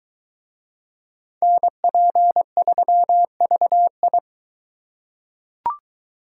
Morse code, heard as a keyed tone, about 22 wpm, 700 Hz. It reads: NP3VI